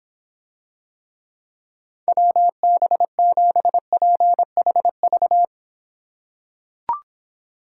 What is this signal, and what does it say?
Morse code, heard as a keyed tone, about 24 wpm, 700 Hz. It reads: WB7PHV